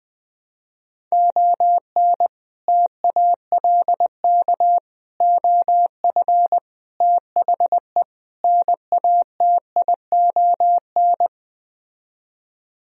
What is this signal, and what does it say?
Morse code, heard as a keyed tone, about 20 wpm, 700 Hz. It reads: ON TALK OF THE NATION